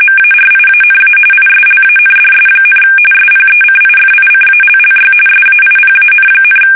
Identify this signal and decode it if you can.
RTTY, about 45.45 baud. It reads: MARIANA ISLANDS, SPECIFICALLY SAIPAN,
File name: fsk850 / battleshipnj